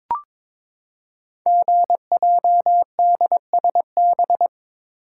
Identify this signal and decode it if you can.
Morse code, heard as a keyed tone, about 22 wpm, 700 Hz. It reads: GJDSB
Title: Code Groups - 5 Letters - 22wpm